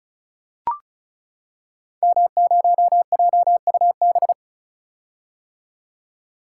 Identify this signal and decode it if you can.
Morse code, heard as a keyed tone, about 36 wpm, 700 Hz. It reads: M0JUB
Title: Code Groups - 5 Letter:Numbers - 35wpm